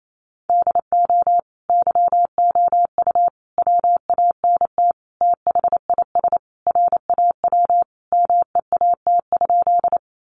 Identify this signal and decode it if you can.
Morse code, heard as a keyed tone, about 28 wpm, 700 Hz. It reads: DO YOU WANT THIS RAW MEAT?
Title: Sentences from Top 1500 Words 28wpm